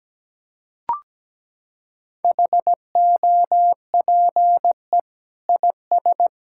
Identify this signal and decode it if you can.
Morse code, heard as a keyed tone, about 17 wpm, 700 Hz. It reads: HOPE IS
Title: Sentences from Top 1000 Words - Review Part 5 - 17wpm